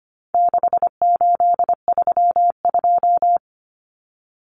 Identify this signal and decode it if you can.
Morse code, heard as a keyed tone, about 25 wpm, 700 Hz. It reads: 6832
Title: Numbers - 4-Digit 25wpm